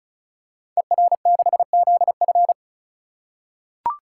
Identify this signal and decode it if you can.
Morse code, heard as a keyed tone, about 35 wpm, 700 Hz. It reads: ER6ZF
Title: Code Groups - 5 Letter:Numbers - 35wpm